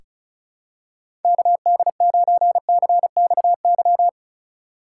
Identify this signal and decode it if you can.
Morse code, heard as a keyed tone, about 36 wpm, 700 Hz. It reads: KD9CXY
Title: USA Call Signs - 6 Characters - 4x Character Spacing - 35wpm